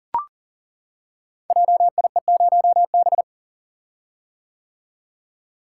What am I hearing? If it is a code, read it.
Morse code, heard as a keyed tone, about 40 wpm, 690 Hz. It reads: JIE0B
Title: Code Groups - 5 Letter:Numbers - 40wpm